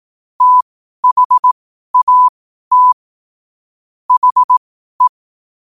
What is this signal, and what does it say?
Morse code, heard as a keyed tone, about 18 wpm, spaced as for 12 wpm, 1000 Hz. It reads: THAT HE
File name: qotdfiles / 12wpm